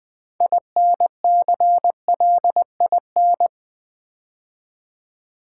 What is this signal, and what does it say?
Morse code, heard as a keyed tone, about 20 wpm, 700 Hz. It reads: INCLIN
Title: Build A Word - 3 Letters At A Time - 8-Letter Words - 20wpm